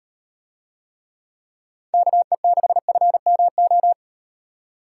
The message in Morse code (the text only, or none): KE6FMO